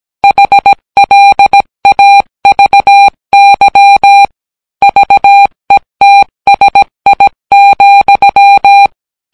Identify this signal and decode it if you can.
Morse code, heard as a keyed tone, about 18 wpm, 800 Hz. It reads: HLAVY VETSI,